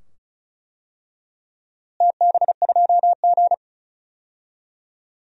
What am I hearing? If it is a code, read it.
Morse code, heard as a keyed tone, about 36 wpm, 700 Hz. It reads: TB2G
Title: DXCC-100 Call Signs - 4 Characters - Repeated 2x - 35wpm